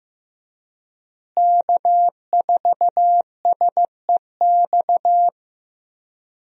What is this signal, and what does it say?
Morse code, heard as a keyed tone, about 15 wpm, 700 Hz. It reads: K4SEX
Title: USA Call Signs - 5 Characters - Repeated 3x - 15wpm